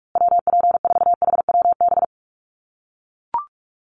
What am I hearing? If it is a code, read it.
Morse code, heard as a keyed tone, about 44 wpm, 700 Hz. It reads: WP4HWB
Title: USA Call Signs Repeated 2x 45wpm